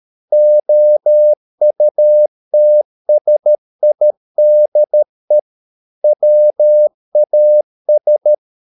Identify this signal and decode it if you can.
Morse code, heard as a keyed tone, about 13 wpm, 600 Hz. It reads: OUTSIDE WAS